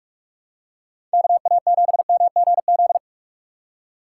Morse code, heard as a keyed tone, about 46 wpm, 700 Hz. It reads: KA7MGZ